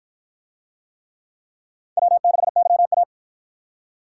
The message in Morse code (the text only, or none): W6YA